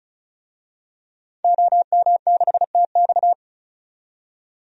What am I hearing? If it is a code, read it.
Morse code, heard as a keyed tone, about 34 wpm, 700 Hz. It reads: OM6TX
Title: DXCC-100 Call Signs - 5 Characters - 35wpm